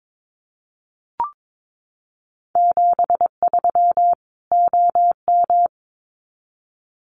Morse code, heard as a keyed tone, about 22 wpm, 700 Hz. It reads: 73 OM
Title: Common QSO Phrases 22wpm